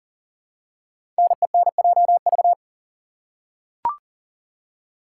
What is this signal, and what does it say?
Morse code, heard as a keyed tone, about 38 wpm, 700 Hz. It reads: NENJV
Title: Code Groups - 5 Letters - 40wpm